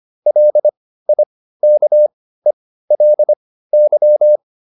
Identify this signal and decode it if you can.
Morse code, heard as a keyed tone, about 25 wpm, spaced as for 15 wpm, 600 Hz. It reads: LIKELY